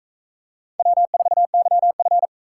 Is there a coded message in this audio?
Yes: Morse code from 0.8 to 2.3 s, heard as a keyed tone, about 38 wpm, 700 Hz. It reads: W4YF